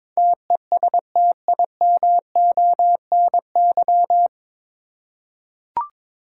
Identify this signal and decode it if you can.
Morse code, heard as a keyed tone, about 21 wpm, 700 Hz. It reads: TESTIMONY
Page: Morse Code Ninja practice 22wpm